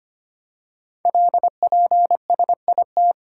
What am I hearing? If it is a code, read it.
Morse code, heard as a keyed tone, about 25 wpm, 700 Hz. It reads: LPSIT
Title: Code Groups - 5 Letters - 25wpm